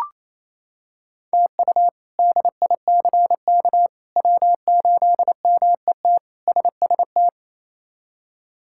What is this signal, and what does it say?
Morse code, heard as a keyed tone, about 28 wpm, 700 Hz. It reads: TU DICK W8MET SST
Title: K1USN SST QSO Practice - 28wpm